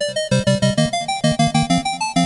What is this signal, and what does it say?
There is no code.